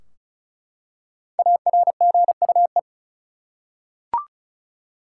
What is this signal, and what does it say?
Morse code, heard as a keyed tone, about 34 wpm, 700 Hz. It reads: ARGUE
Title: Five Letter Words - 35wpm